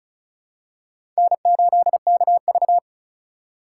Morse code, heard as a keyed tone, about 34 wpm, 700 Hz. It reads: N8KV
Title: USA Call Signs - 4 Characters - 4x Character Spacing - 35wpm